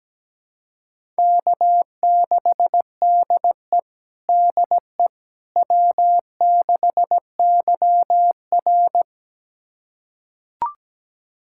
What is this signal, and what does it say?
Morse code, heard as a keyed tone, about 17 wpm, 700 Hz. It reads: K6DE DE W6YR
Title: USA Callsign DE Callsign - 4 Characters - 17wpm